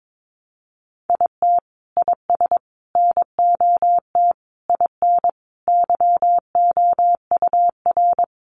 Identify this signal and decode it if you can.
Morse code, heard as a keyed tone, about 22 wpm, 700 Hz. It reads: IT IS NOT IN YOUR
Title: Sentences from Top 1400 Words 22wpm